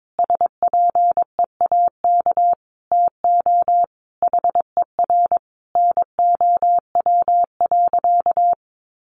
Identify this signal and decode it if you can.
Morse code, heard as a keyed tone, about 22 wpm, 700 Hz. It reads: SPEAK TO HER NOW.
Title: Sentences from Top 1400 Words 22wpm